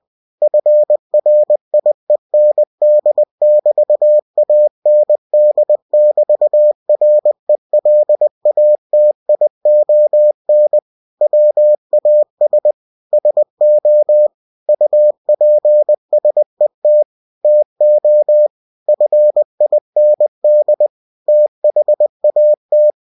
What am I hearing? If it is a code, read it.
Morse code, heard as a keyed tone, about 20 wpm, 600 Hz. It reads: FRIEND<BT>AND<BT>RELATION WAS SO UPSET TO FIND THAT